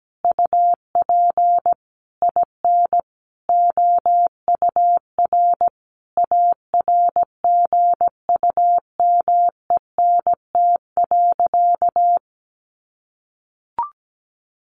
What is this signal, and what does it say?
Morse code, heard as a keyed tone, about 17 wpm, 700 Hz. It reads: UP IN OUR ARGUMENT.
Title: Sentences from Top 1500 Words 17wpm